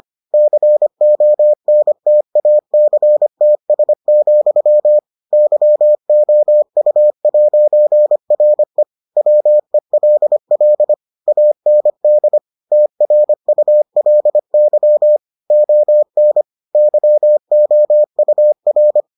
Morse code, heard as a keyed tone, about 25 wpm, 600 Hz. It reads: CONTACTS, YOU'RE WELL AND TRULY ON YOUR